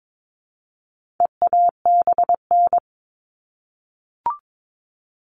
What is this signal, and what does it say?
Morse code, heard as a keyed tone, about 22 wpm, 700 Hz. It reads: EABN
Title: All Cut Numbers - 4-Digit 22wpm